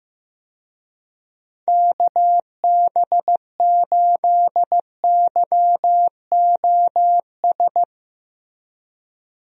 Morse code, heard as a keyed tone, about 16 wpm, 700 Hz. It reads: KB8YOS
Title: USA Call Signs - 6 Characters - Repeated 2x - 15wpm